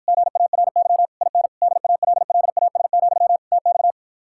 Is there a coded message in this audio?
Yes: Morse code from 0.1 to 3.9 s, heard as a keyed tone, about 52 wpm, 700 Hz. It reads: GARY IN DALLAS, TX